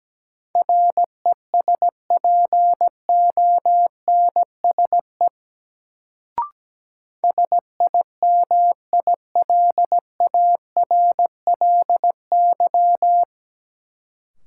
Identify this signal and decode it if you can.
Morse code, heard as a keyed tone, about 17 wpm, 700 Hz. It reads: RESPONSE SIMILARLY
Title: Sentences from Top 1400 Words - New Word Drill Alt - 17wpm